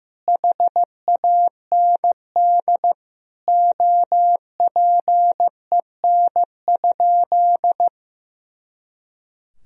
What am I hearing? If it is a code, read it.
Morse code, heard as a keyed tone, about 15 wpm, 700 Hz. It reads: HAND OPEN?
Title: Sentences from Top 300 Words and Abbreviations 15wpm